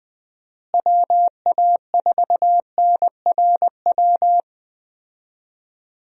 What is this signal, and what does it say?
Morse code, heard as a keyed tone, about 20 wpm, 700 Hz. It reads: WA4NRW
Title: USA Call Signs - 6 Characters - Repeated 3x - 20wpm